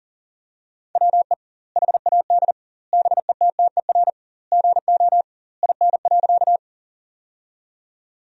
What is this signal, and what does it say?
Morse code, heard as a keyed tone, about 40 wpm, 700 Hz. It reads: WE HAD BETTER GO IN.
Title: 2x Word-Spacing - Sentences from Top 400 Words - 40wpm